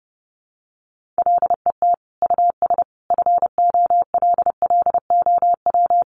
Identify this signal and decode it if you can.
Morse code, heard as a keyed tone, about 28 wpm, 700 Hz. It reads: LET US FOLLOW
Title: Sentences from Top 1400 Words 30wpm